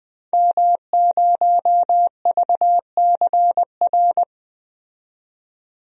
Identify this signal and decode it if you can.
Morse code, heard as a keyed tone, about 20 wpm, 700 Hz. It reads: M0VCR
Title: DXCC-100 Call Signs - 5 Characters - 20wpm